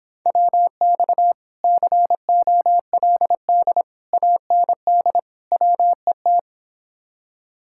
Morse code, heard as a keyed tone, about 26 wpm, 700 Hz. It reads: WX COLD AND WET